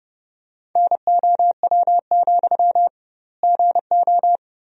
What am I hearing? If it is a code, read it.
Morse code, heard as a keyed tone, about 30 wpm, 700 Hz. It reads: NOW, GO
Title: 2x Word-Spacing - Sentences from Top 200 Words - 30wpm